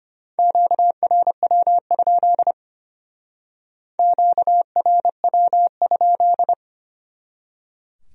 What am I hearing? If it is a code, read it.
Morse code, heard as a keyed tone, about 33 wpm, 700 Hz. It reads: QRW? QRW?